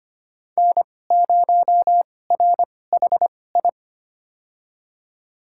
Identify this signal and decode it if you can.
Morse code, heard as a keyed tone, about 25 wpm, spaced as for 18 wpm, 700 Hz. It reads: N0RHI
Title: USA Call Signs - 5 Characters - 2x Character Spacing - 25wpm